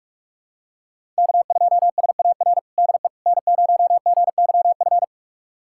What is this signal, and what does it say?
Morse code, heard as a keyed tone, about 42 wpm, 700 Hz. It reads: K2SAR DE N0GYF
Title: USA Callsign DE Callsign - 5 Characters - 45wpm